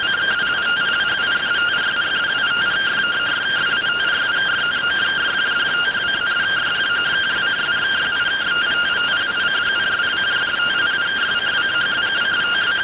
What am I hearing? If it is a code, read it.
Two-tone teleprinter signal, about 170 Hz shift.